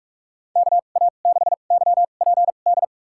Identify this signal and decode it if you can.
Morse code, heard as a keyed tone, about 46 wpm, spaced as for 32 wpm, 700 Hz. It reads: KA6YPD
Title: USA Call Signs - 6 Characters - 2x Character Spacing - 45wpm